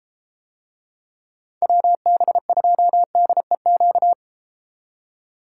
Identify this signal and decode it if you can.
Morse code, heard as a keyed tone, about 31 wpm, 700 Hz. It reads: WB2DEQ